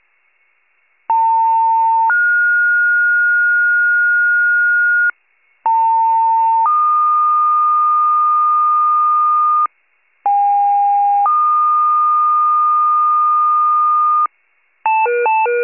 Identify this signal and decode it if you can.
Two-tone page, heard as a one-second tone, then a three-second tone, about 900 then 1430 Hz, about 900 then 1220 Hz, about 810 then 1220 Hz. It is heard from 1.1 to 14.3 s.
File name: wcalert